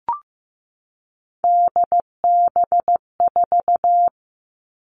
Morse code, heard as a keyed tone, about 15 wpm, 700 Hz. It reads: DB4
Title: Three Challenging Letter-Number Combinations 15wpm